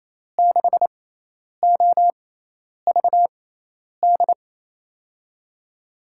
Morse code, heard as a keyed tone, about 28 wpm, spaced as for 10 wpm, 700 Hz. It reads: 6OVD
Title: USA Call Signs - 6 Characters - 6x Character Spacing - 28wpm